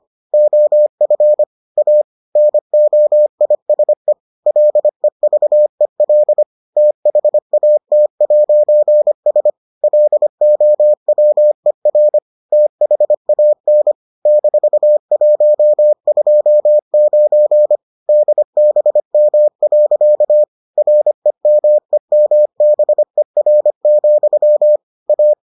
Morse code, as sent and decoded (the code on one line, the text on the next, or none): --- ..-. / .- / -. --- .. ... / .-.. ...- . .-.. / - .... .- - .----. ... / .-.. --- .-- . .-. / - .... .- -. / -....- .---- ..--- ----. / -.. -... -- .-.-.- / .-. -- . -- -... .-. --..-- / .-
OF A NOISE LEVEL THAT'S LOWER THAN -129 DBM. REMEMBER, A